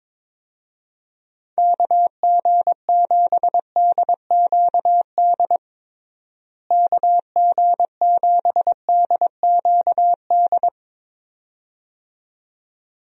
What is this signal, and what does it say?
Morse code, heard as a keyed tone, about 22 wpm, 700 Hz. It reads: KG7DQD KG7DQD
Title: USA Call Signs - 6 Characters - Repeated 2x - 22wpm